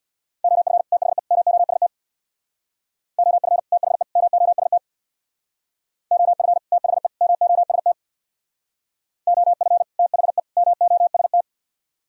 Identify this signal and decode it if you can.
Morse code, heard as a keyed tone, about 61 wpm, 700 Hz. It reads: OF THE MOST OF THE MOST OF THE MOST OF THE MOST
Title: Speed-Racing - Sets of 3 Words 50wpm